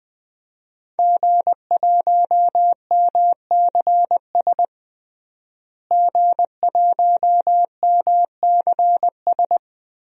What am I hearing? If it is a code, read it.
Morse code, heard as a keyed tone, about 20 wpm, 700 Hz. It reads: G1MCS G1MCS
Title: DXCC-100 Call Signs - 5 Characters - Repeated 3x - 20wpm